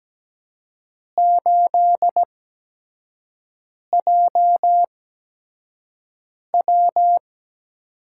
Morse code, heard as a keyed tone, about 17 wpm, spaced as for 5 wpm, 700 Hz. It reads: 8JW
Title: USA Call Signs - 4 Characters - 8x Character Spacing - 17wpm